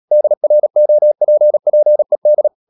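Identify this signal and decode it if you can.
Morse code, heard as a keyed tone, about 37 wpm, 600 Hz. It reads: DROPPED
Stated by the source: Morse code practice at 35–39 wpm sending English text of Ken Follett's novel Never, from CW med Gnister.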